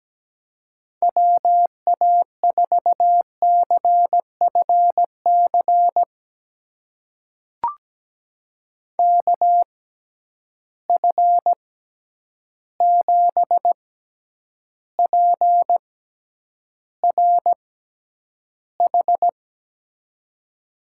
Morse code, heard as a keyed tone, about 17 wpm, spaced as for 9 wpm, 700 Hz. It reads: WA4CFC KF7PRH
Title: USA Call Signs - 6 Characters - 6x Character Spacing - 17wpm